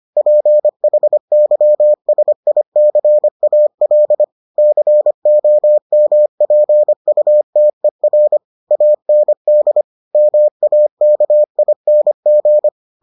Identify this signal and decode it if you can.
Morse code, heard as a keyed tone, about 25 wpm, 600 Hz. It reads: PHYSICAL COMPUTER AND MAKING